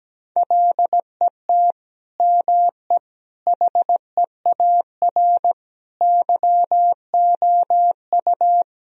Morse code, heard as a keyed tone, about 17 wpm, 700 Hz. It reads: LET ME HEAR YOU
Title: Sentences from Top 1000 Words - Review Part 2 - 17wpm